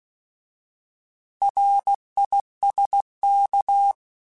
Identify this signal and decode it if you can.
Morse code, heard as a keyed tone, about 16 wpm, 800 Hz. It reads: RISK
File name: Alphabetical / Alphabetical mp3s / Rs @15